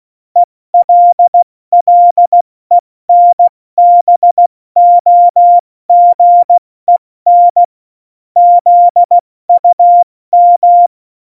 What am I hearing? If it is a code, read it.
Morse code, heard as a keyed tone, about 16 wpm, 700 Hz. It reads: ELLENBOGEN ZUM
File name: Morsekurs / Witze / Witz 13. November 2018, 70 F80